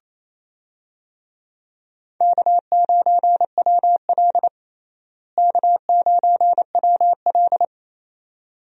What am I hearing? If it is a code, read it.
Morse code, heard as a keyed tone, about 28 wpm, 700 Hz. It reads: K9WL K9WL
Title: USA Call Signs - 4 Characters - Repeated 2x - 28wpm